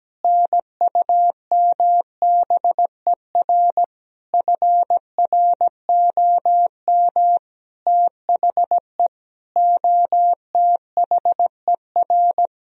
Morse code, heard as a keyed tone, about 17 wpm, 700 Hz. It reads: NUMBER FROM THE OTHER
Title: Sentences from Top 1000 Words - Review Part 1 - 17wpm